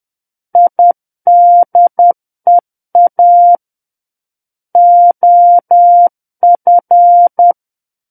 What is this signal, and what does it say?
Morse code, heard as a keyed tone, about 10 wpm, 700 Hz. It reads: IDEA OF
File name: Book One / 01-Chapter 1 - 10 WPM